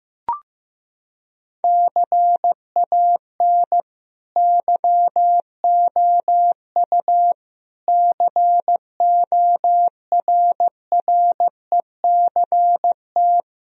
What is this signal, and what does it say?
Morse code, heard as a keyed tone, about 16 wpm, 700 Hz. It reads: CAN YOU CORRECT
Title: Sentences from Top 1000 Words - Review Part 2 - 15wpm